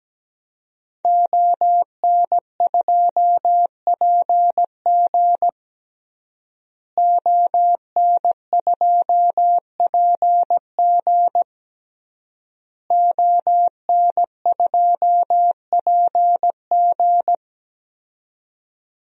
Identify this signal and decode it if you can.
Morse code, heard as a keyed tone, about 17 wpm, 700 Hz. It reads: ON2PG ON2PG ON2PG
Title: DXCC-100 Call Signs - 5 Characters - Repeated 3x - 17wpm